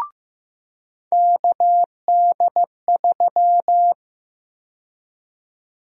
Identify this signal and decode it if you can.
Morse code, heard as a keyed tone, about 15 wpm, 700 Hz. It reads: KD3
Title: USA Call Sign Prefixes - 15wpm